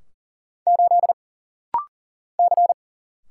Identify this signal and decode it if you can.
Morse code, heard as a keyed tone, about 40 wpm, 700 Hz. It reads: 8 C
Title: Single Letter-Number - Rapid-Fire 40wpm